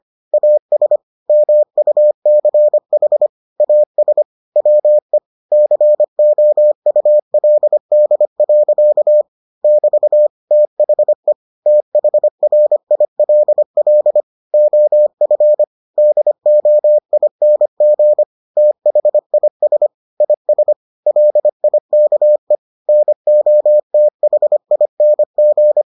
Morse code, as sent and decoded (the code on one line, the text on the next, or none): .- ... / -- ..- -.-. .... / .- ... / .-- . / -.-. --- ..- .-.. -.. .-.-.- / -...- / - .... / - .... .-. .. .-.. .-.. / --- ..-. / -.. --- .. -. --. / - .... .. ... / .. ... / .-.. .. -.- . / -. --- - .... .. -. --.
AS MUCH AS WE COULD. <BT> THE THRILL OF DOING THIS IS LIKE NOTHING